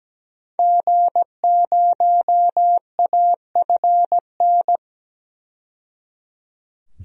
Morse code, heard as a keyed tone, about 17 wpm, 700 Hz. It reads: G0AFN
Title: DXCC-100 Call Signs - 5 Characters - 17wpm